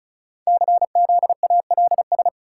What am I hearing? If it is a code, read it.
Morse code, heard as a keyed tone, about 36 wpm, 690 Hz. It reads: CZALS
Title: Code Groups - 5 Letters - 35wpm